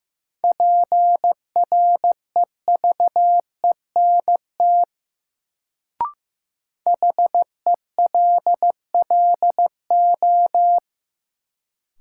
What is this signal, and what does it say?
Morse code, heard as a keyed tone, about 15 wpm, 700 Hz. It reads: PREVENT HELLO
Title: Sentences from Top 1100 Words - New Word Drill Alt - 15wpm